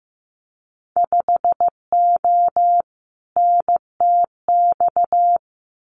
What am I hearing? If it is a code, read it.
Morse code, heard as a keyed tone, about 15 wpm, 700 Hz. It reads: 5O NTX
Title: Winter Field Day Exchanges Encore 15wpm